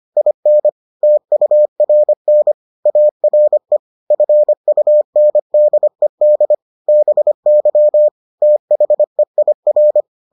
Morse code, heard as a keyed tone, about 25 wpm, 600 Hz. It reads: IN TURN ARE FUNDED BY THEIR